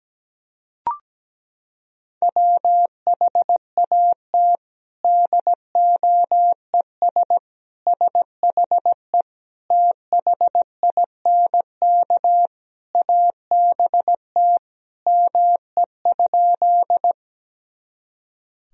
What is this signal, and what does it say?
Morse code, heard as a keyed tone, about 17 wpm, 700 Hz. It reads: WHAT DOES SHE THINK ABT ME?
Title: Sentences from Top 200 Words and Abbreviations 17wpm